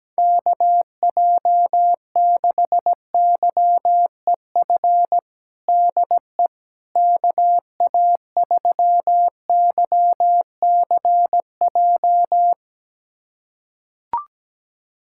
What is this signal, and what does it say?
Morse code, heard as a keyed tone, about 17 wpm, 700 Hz. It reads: KJ6YEF DE KA3YCJ
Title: USA Callsign DE Callsign 17wpm